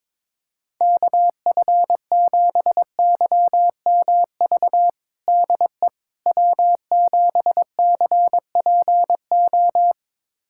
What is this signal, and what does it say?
Morse code, heard as a keyed tone, about 22 wpm, 700 Hz. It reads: KF7YMV DE W7CPO
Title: USA Callsign DE Callsign 22wpm